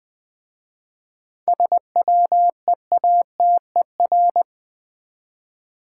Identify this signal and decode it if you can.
Morse code, heard as a keyed tone, about 20 wpm, 700 Hz. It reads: SWEATER
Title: Easy Words with Hints - 20wpm